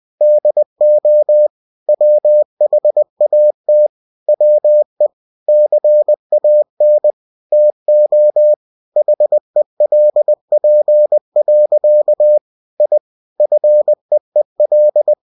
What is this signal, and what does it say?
Morse code, heard as a keyed tone, about 20 wpm, 600 Hz. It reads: DO WHAT WE CAN TO HELP. I FEEL
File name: cw-books / winnie-the-pooh / winnie-20wpm-page-0107